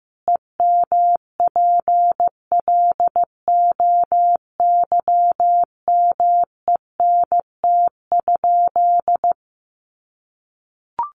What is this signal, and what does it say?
Morse code, heard as a keyed tone, about 15 wpm, 700 Hz. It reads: EMPLOYMENT?
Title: Sentences from Top 1500 Words 15wpm